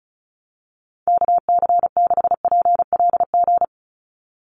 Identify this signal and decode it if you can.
Morse code, heard as a keyed tone, about 36 wpm, 700 Hz. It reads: KC6PLG
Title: USA Call Signs 35wpm